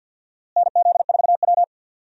Morse code, heard as a keyed tone, about 52 wpm, 700 Hz. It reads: NZ4W
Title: USA Call Signs - 4 Characters - Repeated 3x - 50wpm